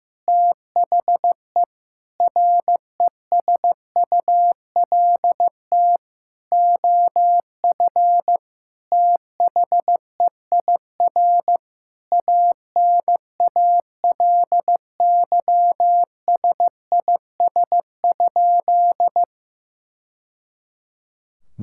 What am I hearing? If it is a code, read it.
Morse code, heard as a keyed tone, about 15 wpm, 700 Hz. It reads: THE RESULT OF THEIR ANALYSIS?
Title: Sentences from Top 1000 Words - Review Part 5 - 15wpm